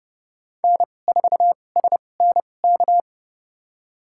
Morse code, heard as a keyed tone, about 30 wpm, spaced as for 22 wpm, 700 Hz. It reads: N4SNK
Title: USA Call Signs - 5 Characters - 2x Character Spacing - 30wpm